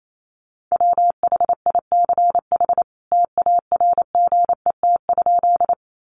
Morse code, heard as a keyed tone, about 28 wpm, 700 Hz. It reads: WHICH TARGET?
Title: Sentences from Top 1200 Words 28wpm